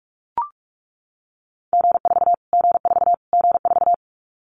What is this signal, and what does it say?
Morse code, heard as a keyed tone, about 44 wpm, 700 Hz. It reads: G4 G4 G4
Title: Callsign Prefixes for Top 25 DXCC - Repeated 3x - 45wpm